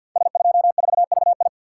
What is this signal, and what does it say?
Morse code, heard as a keyed tone, about 50 wpm, 700 Hz. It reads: S24VI
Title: Code Groups - 5 Letter:Numbers - 50wpm